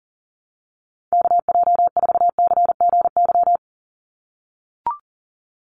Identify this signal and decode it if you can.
Morse code, heard as a keyed tone, about 40 wpm, 700 Hz. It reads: KJ4CGY
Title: USA Call Signs 40wpm